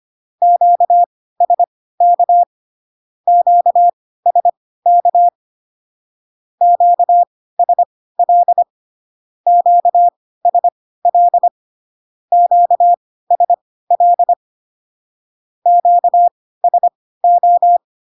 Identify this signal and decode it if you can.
Morse code, heard as a keyed tone, about 25 wpm, spaced as for 16 wpm, 700 Hz. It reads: QSK QSK QSL QSL QSL QSO
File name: cw / Q-codes most common repeated 3 times 10 25wpm ik1hge